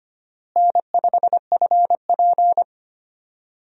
Morse code, heard as a keyed tone, about 25 wpm, 700 Hz. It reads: N5FP
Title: USA Call Signs - 4 Characters - Repeated 2x - 25wpm